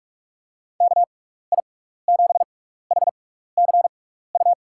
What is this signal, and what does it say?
Morse code, heard as a keyed tone, about 44 wpm, spaced as for 15 wpm, 700 Hz. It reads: KI7HCU
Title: USA Call Signs - 6 Characters - 6x Character Spacing - 45wpm